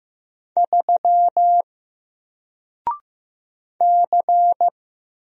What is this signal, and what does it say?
Morse code, heard as a keyed tone, about 15 wpm, 700 Hz. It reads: 3 C
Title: Single Character (Letter:Number:Punctuation:Prosign) Alt 15wpm